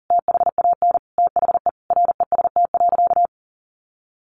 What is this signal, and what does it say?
Morse code, heard as a keyed tone, about 40 wpm, 700 Hz. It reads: THAN THE REST.
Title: Sentences from Top 1400 Words 40wpm